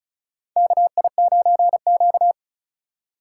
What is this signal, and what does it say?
Morse code, heard as a keyed tone, about 34 wpm, 700 Hz. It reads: KI9Q
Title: USA Call Signs - 4 Characters - 4x Character Spacing - 35wpm